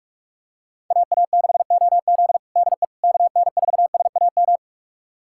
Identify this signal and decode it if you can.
Morse code, heard as a keyed tone, about 47 wpm, 700 Hz. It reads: AA6OZ DE KN4SAM
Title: USA Callsign DE Callsign 45wpm